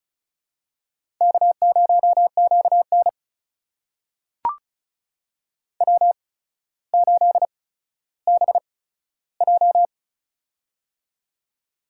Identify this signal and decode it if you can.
Morse code, heard as a keyed tone, about 36 wpm, spaced as for 15 wpm, 700 Hz. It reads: K0QN W8BJ
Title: USA Call Signs - 4 Characters - 8x Character Spacing - 35wpm